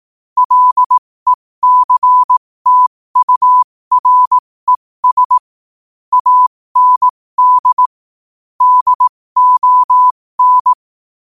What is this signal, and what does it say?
Morse code, heard as a keyed tone, about 18 wpm, spaced as for 15 wpm, 1000 Hz. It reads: LECTURES AND DON
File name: qotdfiles / 15wpm